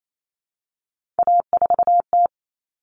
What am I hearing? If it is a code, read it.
Morse code, heard as a keyed tone, about 28 wpm, 700 Hz. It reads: A4T
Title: All Cut Numbers - 3-Digit 28wpm